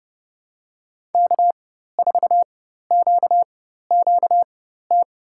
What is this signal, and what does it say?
Morse code, heard as a keyed tone, about 30 wpm, spaced as for 14 wpm, 700 Hz. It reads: K4QQT
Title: USA Call Signs - 5 Characters - 4x Character Spacing - 30wpm